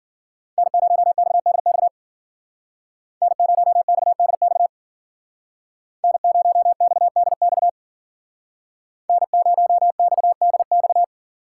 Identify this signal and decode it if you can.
Morse code, heard as a keyed tone, about 51 wpm, 700 Hz. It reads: N0XDX N0XDX N0XDX N0XDX